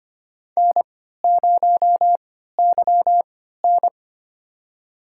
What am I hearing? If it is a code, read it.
Morse code, heard as a keyed tone, about 25 wpm, spaced as for 14 wpm, 700 Hz. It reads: N0YN
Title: USA Call Signs - 3x Character Spacing - 25wpm